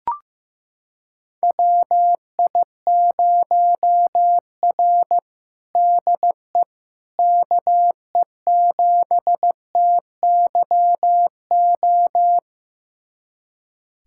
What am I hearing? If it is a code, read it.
Morse code, heard as a keyed tone, about 15 wpm, 700 Hz. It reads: WI0R DE KE7TYO
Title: USA Callsign DE Callsign 15wpm